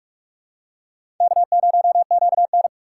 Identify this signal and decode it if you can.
Morse code, heard as a keyed tone, about 47 wpm, 700 Hz. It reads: K0QN
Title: USA Call Signs - 4 Characters - 3x Character Spacing - 45wpm